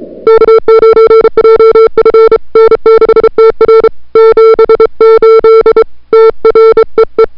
Morse code, heard as a keyed tone, about 34 wpm, 440 Hz. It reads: K9JF N6TR 78 TREE
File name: n6tr / Fab2007CWSprintHour1